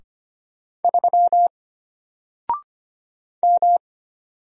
Morse code, heard as a keyed tone, about 25 wpm, 700 Hz. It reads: 3 M
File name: Single-Letter-Number-25wpm